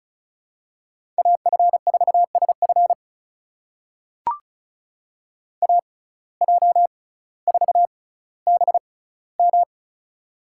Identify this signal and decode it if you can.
Morse code, heard as a keyed tone, about 36 wpm, spaced as for 18 wpm, 700 Hz. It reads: AF4SF AJ4BM
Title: USA Call Signs - 5 Characters - 6x Character Spacing - 35wpm